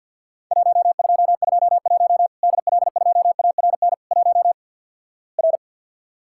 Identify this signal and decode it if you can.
Morse code, heard as a keyed tone, about 52 wpm, 700 Hz. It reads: 1221 DL2ARN 1 R
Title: Worked All Europe DX Contest - QTC Exchange with Repeats - 50wpm